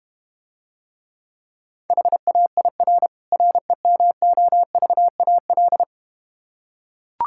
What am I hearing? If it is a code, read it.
Morse code, heard as a keyed tone, about 32 wpm, 700 Hz. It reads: HAIR REMOVAL